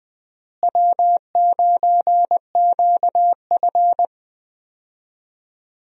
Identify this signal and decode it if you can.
Morse code, heard as a keyed tone, about 20 wpm, 700 Hz. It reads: W9QF
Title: USA Call Signs - 4 Characters - Repeated 3x - 20wpm